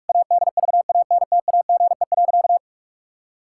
Morse code, heard as a keyed tone, about 43 wpm, 700 Hz. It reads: ADVANTAGE.
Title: Sentences from Top 1000 Words - Review Part 6 - 45wpm